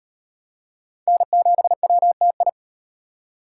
Morse code, heard as a keyed tone, about 38 wpm, 700 Hz. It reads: N7WTI